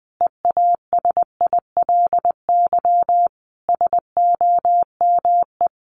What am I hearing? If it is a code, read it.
Morse code, heard as a keyed tone, about 20 wpm, 700 Hz. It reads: EASILY SOME
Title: Sentences from Top 900 Words 20wpm